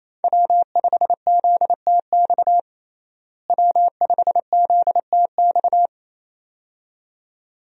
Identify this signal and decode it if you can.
Morse code, heard as a keyed tone, about 28 wpm, 700 Hz. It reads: W5ZTX W5ZTX
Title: USA Call Signs - 5 Characters - Repeated 2x - 28wpm